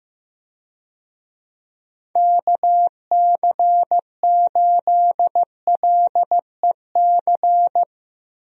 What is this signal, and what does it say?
Morse code, heard as a keyed tone, about 15 wpm, 700 Hz. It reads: KC8LEC
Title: USA Call Signs - 6 Characters - Repeated 3x - 15wpm